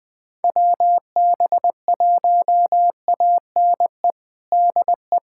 Morse code, heard as a keyed tone, about 20 wpm, 700 Hz. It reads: WB1ANE DE
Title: USA Callsign DE Callsign - 6 Characters - 20wpm